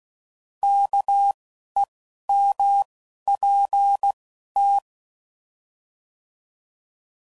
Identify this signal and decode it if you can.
Morse code, heard as a keyed tone, about 16 wpm, 800 Hz. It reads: K E M P T